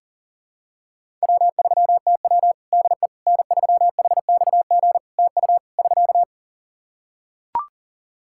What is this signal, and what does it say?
Morse code, heard as a keyed tone, about 40 wpm, 700 Hz. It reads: W3TW DE N3HXG TU <SK>
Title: Practice QSO - Exchange 4 - Ending - 40wpm